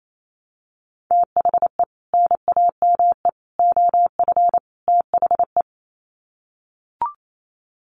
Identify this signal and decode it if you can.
Morse code, heard as a keyed tone, about 28 wpm, 700 Hz. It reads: THE NAME OF THE
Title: Sets of 4 Words 28wpm